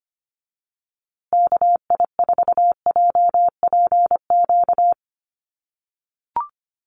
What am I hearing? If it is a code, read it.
Morse code, heard as a keyed tone, about 25 wpm, 700 Hz. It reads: KI4JPQ
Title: USA Call Signs Repeated 3x 25wpm